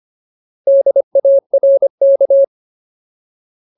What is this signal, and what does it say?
Morse code, heard as a keyed tone, about 25 wpm, 550 Hz. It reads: DARK
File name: Ultimate Word List / 25WPM Ultimate